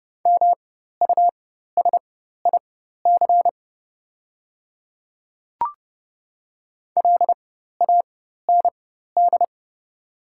Morse code, heard as a keyed tone, about 30 wpm, spaced as for 14 wpm, 700 Hz. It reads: MUSIC LAND
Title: 4x Character Spacing - Top 500 Words - 30wpm